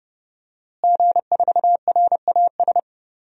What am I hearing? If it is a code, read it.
Morse code, heard as a keyed tone, about 30 wpm, 700 Hz. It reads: G4RAS
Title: DXCC-100 Call Signs - 5 Characters - Repeated 2x - 30wpm